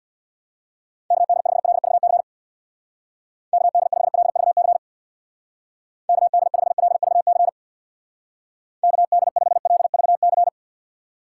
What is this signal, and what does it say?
Morse code, heard as a keyed tone, about 63 wpm, 700 Hz. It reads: KD5LVC KD5LVC KD5LVC KD5LVC